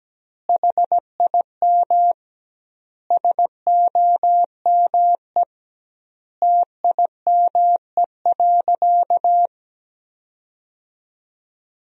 Morse code, heard as a keyed tone, about 17 wpm, 700 Hz. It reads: HIM SOME TIME.
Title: 2x Word-Spacing - Sentences from Top 300 Words - 17wpm